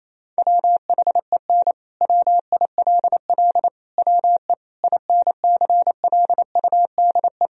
Morse code, heard as a keyed tone, about 28 wpm, 700 Hz. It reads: WHEN WILL WE INCLUDE
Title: Sentences from Top 1000 Words - Review Part 3 - 28wpm